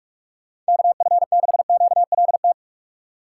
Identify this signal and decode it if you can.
Morse code, heard as a keyed tone, about 64 wpm, spaced as for 54 wpm, 700 Hz. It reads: KF6QLT
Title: Speed Racing - USA Call Signs 45wpm